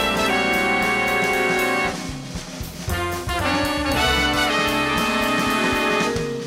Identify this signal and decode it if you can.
Dial-up modem handshake.